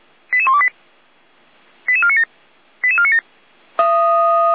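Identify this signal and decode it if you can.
Five-tone selective call (ZVEI), heard as quick runs of five tones, tones of about 70 ms. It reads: 88128 88497 88497
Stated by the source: ZVEI five-tone selective call, code 88129, received by SDR; fire brigade Feuerwehr Orlamünde.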